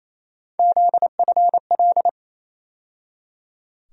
Morse code, heard as a keyed tone, about 28 wpm, 700 Hz. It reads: ZFL
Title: USA Call Sign Postfixes - 3 Letters 28wpm